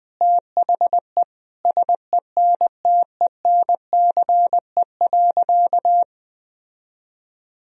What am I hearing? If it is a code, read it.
Morse code, heard as a keyed tone, about 20 wpm, 700 Hz. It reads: THE SENTENCE.
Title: Sentences from Top 1000 Words - Review Part 1 - 20wpm